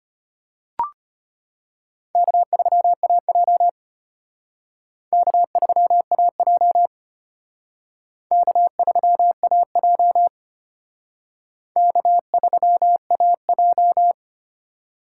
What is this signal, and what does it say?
Morse code, heard as a keyed tone, about 31 wpm, 700 Hz. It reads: K3AJ K3AJ K3AJ K3AJ
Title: Speed Racing - USA 1x2 Call Signs 25wpm